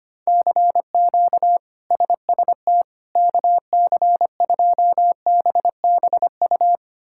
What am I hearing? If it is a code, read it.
Morse code, heard as a keyed tone, about 25 wpm, 700 Hz. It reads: CQ SST KC2BBU
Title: K1USN SST QSO Practice - 25wpm